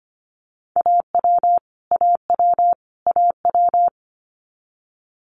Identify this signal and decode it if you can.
Morse code, heard as a keyed tone, about 25 wpm, 700 Hz. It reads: AW AW AW
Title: Multi-letter Phonemes Repeated 3x 25wpm